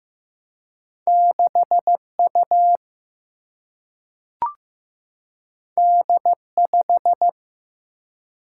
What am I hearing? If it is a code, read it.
Morse code, heard as a keyed tone, about 15 wpm, 700 Hz. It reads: 6 U D 5
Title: Single-Letter Single-Number 15wpm